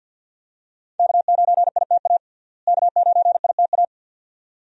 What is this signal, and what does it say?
Morse code, heard as a keyed tone, about 47 wpm, 700 Hz. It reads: K9ITA K9ITA
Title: USA Call Signs - 5 Characters - Repeated 2x - 50wpm